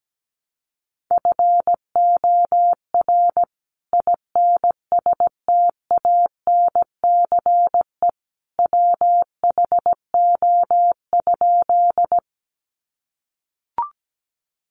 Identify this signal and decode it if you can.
Morse code, heard as a keyed tone, about 17 wpm, 700 Hz. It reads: FOR INSTANCE WHO?
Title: Sentences from Top 800 Words 17wpm